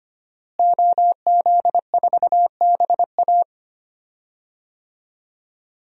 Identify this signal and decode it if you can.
Morse code, heard as a keyed tone, about 25 wpm, 700 Hz. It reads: OZ4BA
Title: DXCC-100 Call Signs - 5 Characters - Repeated 3x - 25wpm